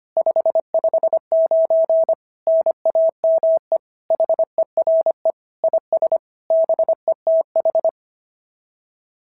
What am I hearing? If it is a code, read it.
Morse code, heard as a keyed tone, about 24 wpm, 650 Hz. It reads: 559 NAME HERE IS BETH
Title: SOTA QSO Practice 25wpm